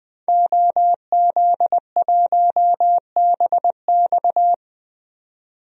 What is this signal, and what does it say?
Morse code, heard as a keyed tone, about 20 wpm, 700 Hz. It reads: OZ1BX
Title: DXCC-100 Call Signs - 5 Characters - 20wpm